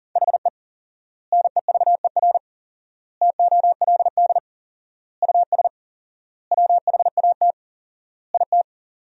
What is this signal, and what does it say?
Morse code, heard as a keyed tone, about 40 wpm, 700 Hz. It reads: HE NEVER TOLD US WHAT IT
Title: 4x Word-Spacing - Sentences from Top 400 Words - 40wpm